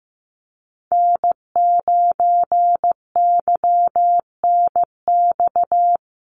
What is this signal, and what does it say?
Morse code, heard as a keyed tone, about 15 wpm, 700 Hz. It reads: N9YNX
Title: USA Call Signs Repeated 2x 15wpm